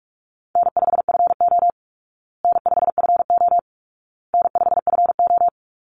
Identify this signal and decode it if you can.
Morse code, heard as a keyed tone, about 44 wpm, 700 Hz. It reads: N5FO N5FO N5FO
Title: Easy Call Signs Repeated 3x - 45wpm